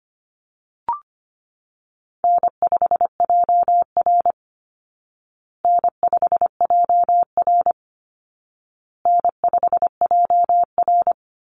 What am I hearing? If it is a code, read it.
Morse code, heard as a keyed tone, about 25 wpm, 700 Hz. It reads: N5JR N5JR N5JR
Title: Easy Call Signs Repeated 3x - 25wpm